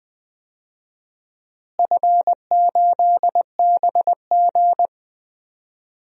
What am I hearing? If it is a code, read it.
Morse code, heard as a keyed tone, about 20 wpm, 700 Hz. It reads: F8BG
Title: DXCC-100 Call Signs - 4 Characters - Repeated 2x - 20wpm